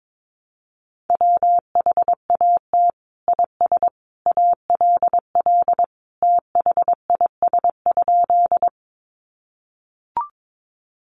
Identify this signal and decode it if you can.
Morse code, heard as a keyed tone, about 22 wpm, 700 Hz. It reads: WHAT IS ALL THIS?
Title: Sentences from Top 100 Words 22wpm